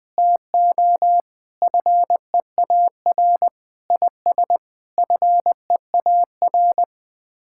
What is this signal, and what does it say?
Morse code, heard as a keyed tone, about 20 wpm, 700 Hz. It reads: TO FEAR IS FEAR
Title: Sets of 9 Words - 20wpm